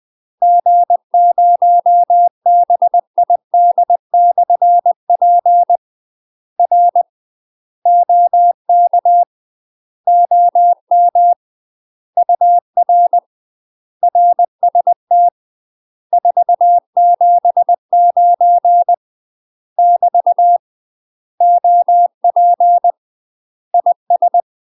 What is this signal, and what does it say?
Morse code, heard as a keyed tone, about 20 wpm, 700 Hz. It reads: G0BID/P R OK OM UR RST 479 <BT> OP IS
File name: UK MASTER QSOs TRAINING / 20WPM / UK MASTER QSOs 0015